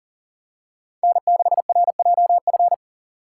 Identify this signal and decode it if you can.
Morse code, heard as a keyed tone, about 40 wpm, 700 Hz. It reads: N6RJF
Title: USA Call Signs - 2x Character Spacing - 40wpm